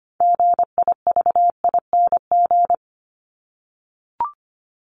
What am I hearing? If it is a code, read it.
Morse code, heard as a keyed tone, about 25 wpm, 700 Hz. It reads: GIVING
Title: ING Suffix - Top 100 Words with Base Words 25wpm